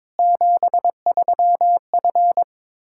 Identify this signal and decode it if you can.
Morse code, heard as a keyed tone, about 22 wpm, 700 Hz. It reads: 73F
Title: Practice QSO - Exchange 2 - Info - 22wpm